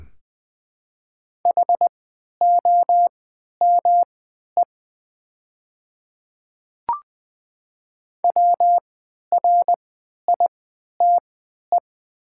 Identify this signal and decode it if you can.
Morse code, heard as a keyed tone, about 20 wpm, spaced as for 11 wpm, 700 Hz. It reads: HOME WRITE
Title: 3x Character Spacing - Top 200 Words - 20wpm